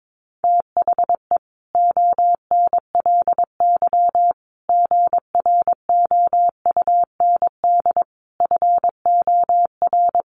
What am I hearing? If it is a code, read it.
Morse code, heard as a keyed tone, about 22 wpm, 700 Hz. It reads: THE ONLY GROUND FOR
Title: Sentences from Top 1400 Words 22wpm